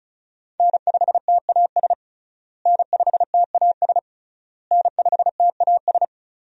Morse code, heard as a keyed tone, about 36 wpm, 700 Hz. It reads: N5TAS N5TAS N5TAS
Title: USA Call Signs - 5 Characters - Repeated 3x - 35wpm